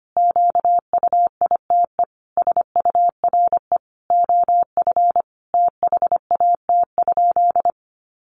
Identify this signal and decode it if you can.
Morse code, heard as a keyed tone, about 25 wpm, 700 Hz. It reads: QUITE SURE OF THAT?
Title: Sentences from Top 700 Words 25wpm